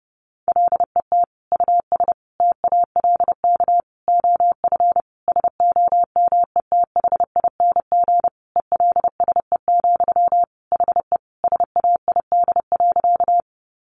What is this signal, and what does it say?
Morse code, heard as a keyed tone, about 30 wpm, 700 Hz. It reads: LET US TALK OF SOMETHING ELSE, HE SAID.
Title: Sentences from Top 800 Words 30wpm